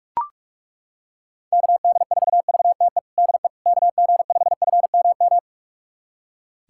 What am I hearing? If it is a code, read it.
Morse code, heard as a keyed tone, about 41 wpm, 700 Hz. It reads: KD4VTE DE KG5FMM
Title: USA Callsign DE Callsign - 6 Characters - 45wpm